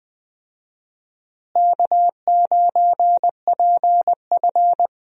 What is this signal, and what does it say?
Morse code, heard as a keyed tone, about 20 wpm, 700 Hz. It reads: K9PF
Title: USA Call Signs - 4 Characters - Repeated 3x - 20wpm